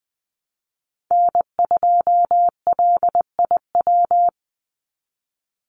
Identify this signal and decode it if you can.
Morse code, heard as a keyed tone, about 20 wpm, 700 Hz. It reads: N2LIW
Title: USA Call Signs 20wpm